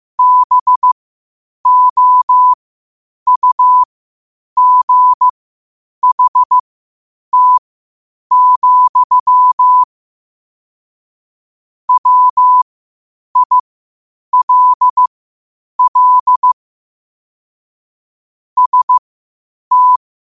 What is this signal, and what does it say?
Morse code, heard as a keyed tone, about 15 wpm, spaced as for 8 wpm, 1000 Hz. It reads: BOUGHT, WILL ST
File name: qotdfiles / 8wpm